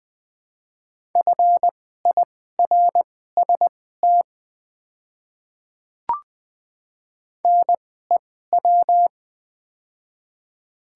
Morse code, heard as a keyed tone, about 20 wpm, spaced as for 14 wpm, 690 Hz. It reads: FIRST NEW
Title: 2x Character Spacing - Top 500 Words - 20wpm